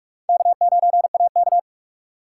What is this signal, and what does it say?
Morse code, heard as a keyed tone, about 46 wpm, 700 Hz. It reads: K9AK
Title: USA Call Signs - 4 Characters - 2x Character Spacing - 45wpm